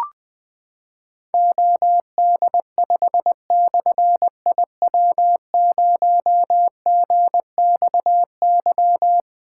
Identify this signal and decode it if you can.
Morse code, heard as a keyed tone, about 20 wpm, 700 Hz. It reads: OD5/IW0GXY
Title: Difficult Call Signs 20wpm